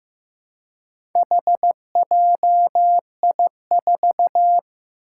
Morse code, heard as a keyed tone, about 15 wpm, 700 Hz. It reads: HJI4
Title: Four Challenging Letter-Number Combinations 15wpm